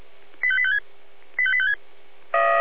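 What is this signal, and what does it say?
Five-tone selective call (ZVEI), heard as quick runs of five tones, tones of about 70 ms. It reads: 85856 85856
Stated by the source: ZVEI five-tone selective call received by SDR, from the Feuerwehr Orlamünde fire brigade audio server.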